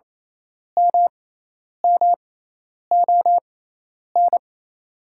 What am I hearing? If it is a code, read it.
Morse code, heard as a keyed tone, about 28 wpm, spaced as for 10 wpm, 700 Hz. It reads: MMON
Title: 6x Character Spacing - Top 500 Words - 28wpm